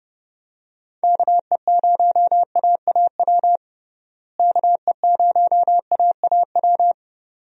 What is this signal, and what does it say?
Morse code, heard as a keyed tone, about 30 wpm, 700 Hz. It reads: KE0AAW KE0AAW
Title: USA Call Signs - 6 Characters - Repeated 2x - 30wpm